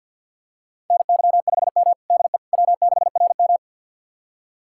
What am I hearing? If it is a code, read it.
Morse code, heard as a keyed tone, about 50 wpm, 690 Hz. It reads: NY5M DE W6RM
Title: USA Callsign DE Callsign - 4 Characters - 50wpm